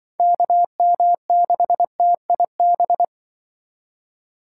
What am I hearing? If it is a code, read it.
Morse code, heard as a keyed tone, about 24 wpm, 700 Hz. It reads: KM6TIB
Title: Speed Racing - USA 2x3 Call Signs 20wpm